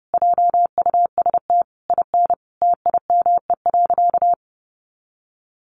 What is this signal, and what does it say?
Morse code, heard as a keyed tone, about 30 wpm, 700 Hz. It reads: JUST IN TIME.
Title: Sentences from Top 300 Words 30wpm